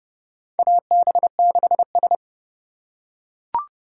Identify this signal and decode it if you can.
Morse code, heard as a keyed tone, about 30 wpm, 700 Hz. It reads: AB6S
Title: USA Call Signs - 4 Characters - 4x Character Spacing - 30wpm